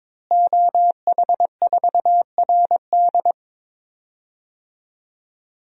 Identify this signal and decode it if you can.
Morse code, heard as a keyed tone, about 22 wpm, 700 Hz. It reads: OH4RD
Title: DXCC-100 Call Signs - 5 Characters - Repeated 3x - 22wpm